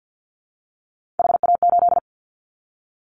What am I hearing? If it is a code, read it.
Morse code, heard as a keyed tone, about 50 wpm, 700 Hz. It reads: HA8
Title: Callsign Prefixes for Top 25 DXCC - 50wpm